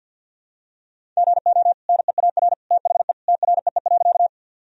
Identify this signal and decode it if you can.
Morse code, heard as a keyed tone, about 46 wpm, 700 Hz. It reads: GO NEAR THE TREE.
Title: Sentences from Top 1000 Words - Review Part 1 - 50wpm